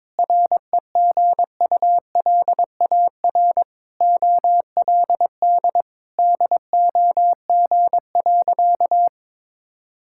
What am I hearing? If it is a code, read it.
Morse code, heard as a keyed tone, about 22 wpm, 700 Hz. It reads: REGULAR OLD DOG.
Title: Sentences from Top 1000 Words - Review Part 6 - 22wpm